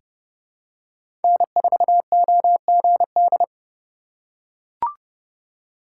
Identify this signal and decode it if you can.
Morse code, heard as a keyed tone, about 30 wpm, 700 Hz. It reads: N4OGD
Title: USA Call Signs - 5 Characters - 2x Character Spacing - 30wpm